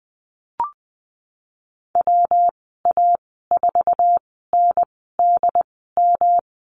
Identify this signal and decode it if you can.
Morse code, heard as a keyed tone, about 20 wpm, spaced as for 15 wpm, 700 Hz. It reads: WA4NDM
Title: 147 Lesson 24g - Letter y - Call Signs 20wpm